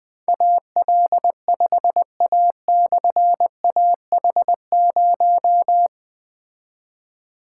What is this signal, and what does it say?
Morse code, heard as a keyed tone, about 20 wpm, 700 Hz. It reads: AL5A/AH0
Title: Difficult Call Signs 20wpm